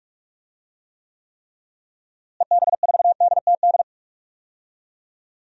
Morse code, heard as a keyed tone, about 46 wpm, 700 Hz. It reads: EB4DTD
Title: DXCC-100 Call Signs - 6 Characters - Repeated 3x - 45wpm